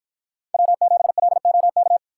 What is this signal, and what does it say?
Morse code, heard as a keyed tone, about 50 wpm, 700 Hz. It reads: W7LOK